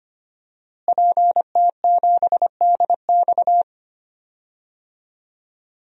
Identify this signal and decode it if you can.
Morse code, heard as a keyed tone, about 25 wpm, 700 Hz. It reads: PT7DX
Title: DXCC-100 Call Signs - 5 Characters - Repeated 3x - 25wpm